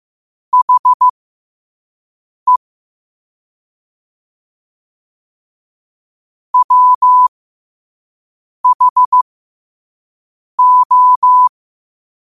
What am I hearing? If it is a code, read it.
Morse code, heard as a keyed tone, about 15 wpm, spaced as for 5 wpm, 1000 Hz. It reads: HE WHO
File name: qotdfiles / 5wpm